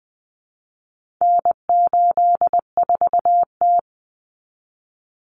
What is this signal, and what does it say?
Morse code, heard as a keyed tone, about 20 wpm, 700 Hz. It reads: N84T
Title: Common Cut Numbers - 4-Digit 20wpm